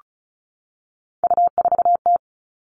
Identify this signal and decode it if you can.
Morse code, heard as a keyed tone, about 34 wpm, 700 Hz. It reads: U4T
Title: All Cut Numbers - 3-Digit 35wpm